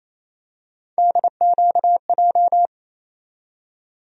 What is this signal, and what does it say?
Morse code, heard as a keyed tone, about 28 wpm, 700 Hz. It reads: DQJ
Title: USA Call Sign Postfixes - 3 Letters 28wpm